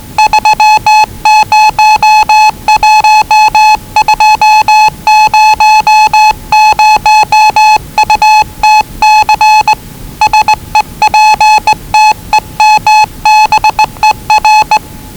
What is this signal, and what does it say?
Morse code, heard as a keyed tone, about 20 wpm, 840 Hz. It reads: 301200UTC SEPTEMBER
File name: WellingtonRadioZlwCloseDownOnHfAndMf / 01-Zlw 4277khz 417.5 qrt until 0925z